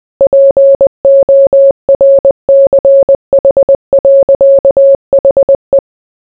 Morse code, heard as a keyed tone, about 20 wpm, 550 Hz. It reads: PORCH. HE